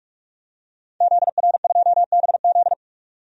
Morse code, heard as a keyed tone, about 44 wpm, 700 Hz. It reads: ZR2BZ
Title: DXCC-100 Call Signs - 5 Characters - 45wpm